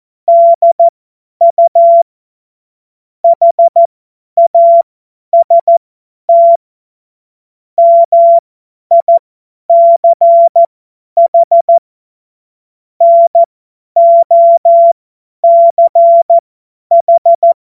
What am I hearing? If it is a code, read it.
Morse code, heard as a keyed tone, about 14 wpm, spaced as for 10 wpm, 680 Hz. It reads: DU HAST MICH NOCH
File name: Morsekurs / Witze / Witz 04. Oktober 2022 50 F70